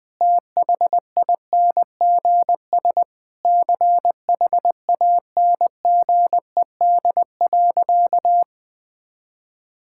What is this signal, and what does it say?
Morse code, heard as a keyed tone, about 20 wpm, 700 Hz. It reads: THINGS CHANGED.
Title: Sentences with 4-Word Phrases - 20wpm